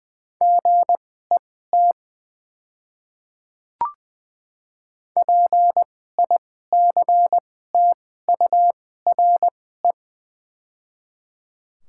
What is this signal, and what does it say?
Morse code, heard as a keyed tone, about 20 wpm, spaced as for 14 wpm, 690 Hz. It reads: GET PICTURE
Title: 2x Character Spacing - Top 200 Words - 20wpm